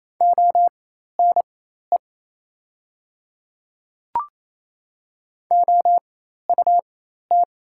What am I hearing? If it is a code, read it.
Morse code, heard as a keyed tone, about 28 wpm, spaced as for 13 wpm, 700 Hz. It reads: ONE OUT
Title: 4x Character Spacing - Top 100 Words - 28wpm